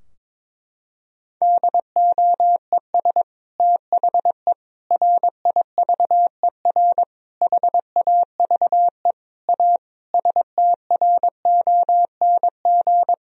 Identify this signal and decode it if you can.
Morse code, heard as a keyed tone, about 21 wpm, 700 Hz. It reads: DOES THE RIVER HAVE A STRONG
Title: Sentences from Top 1000 Words - Review Part 4 - 22wpm